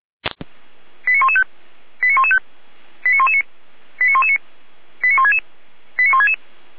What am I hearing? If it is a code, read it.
Five-tone selective call (ZVEI), heard as quick runs of five tones, tones of about 70 ms. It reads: 89115 89115 89119 89119 89166 89166